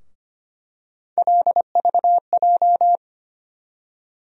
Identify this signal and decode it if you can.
Morse code, heard as a keyed tone, about 25 wpm, 700 Hz. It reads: LVJ
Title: USA Call Sign Postfixes - 3 Letters 25wpm